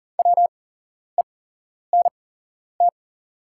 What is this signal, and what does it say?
Morse code, heard as a keyed tone, about 40 wpm, spaced as for 11 wpm, 690 Hz. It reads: WENT
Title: 8x Character Spacing - Top 200 Words - 40wpm